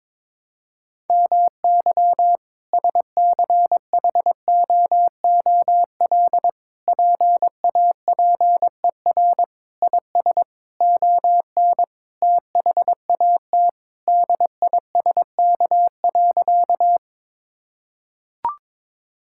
Morse code, heard as a keyed tone, about 22 wpm, 700 Hz. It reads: MY SCHOOL PAPER IS ON THAT DISK.
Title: Sentences from Top 1000 Words - Review Part 6 - 22wpm